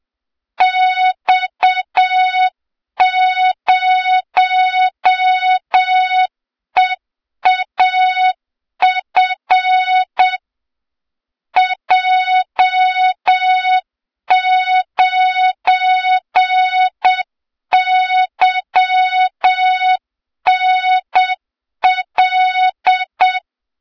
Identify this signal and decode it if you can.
Morse code, heard as a keyed tone, about 7 wpm, 3700 Hz, with noise. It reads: X0EAF J9YNL
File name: Morze-HA2MN / hang / 23 GYAKORLAT